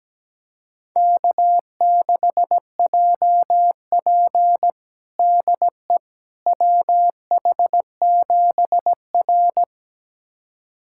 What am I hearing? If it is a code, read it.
Morse code, heard as a keyed tone, about 17 wpm, 700 Hz. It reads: K6JP DE WH7R
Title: USA Callsign DE Callsign - 4 Characters - 17wpm